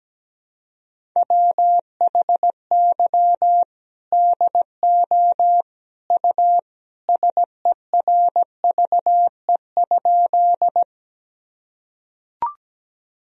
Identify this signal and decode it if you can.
Morse code, heard as a keyed tone, about 17 wpm, 700 Hz. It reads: WHY DO U SERVE?
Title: Sentences from Top 400 Words and Abbreviations 17wpm